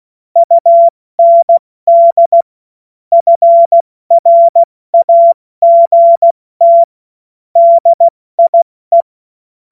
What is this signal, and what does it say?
Morse code, heard as a keyed tone, about 16 wpm, 680 Hz. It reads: UND FRAGT DIE
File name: Morsekurs / Witze / Witz 10. August 2021, 70 F80